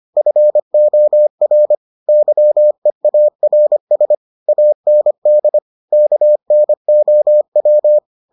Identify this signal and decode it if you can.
Morse code, heard as a keyed tone, about 25 wpm, 600 Hz. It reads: FOR YEARS AND KNOW